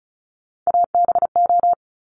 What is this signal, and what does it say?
Morse code, heard as a keyed tone, about 36 wpm, 700 Hz. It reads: ABO
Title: Top Three Letter Combinations in Words 35wpm